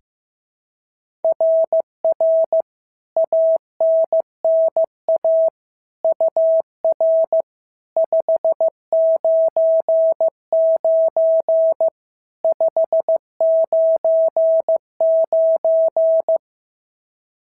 Morse code, heard as a keyed tone, about 14 wpm, 650 Hz. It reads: RR ANNA UR 599 599
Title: SOTA QSO Practice 15wpm